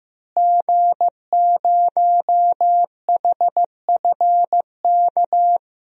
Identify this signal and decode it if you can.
Morse code, heard as a keyed tone, about 16 wpm, 700 Hz. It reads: G0HFK
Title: DXCC-100 Call Signs - 5 Characters - Repeated 3x - 15wpm